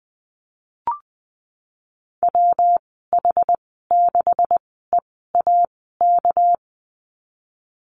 Morse code, heard as a keyed tone, about 20 wpm, 700 Hz. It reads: W H 6 E A K
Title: 238 Lesson 38g - Letter q - Call Signs 20wpm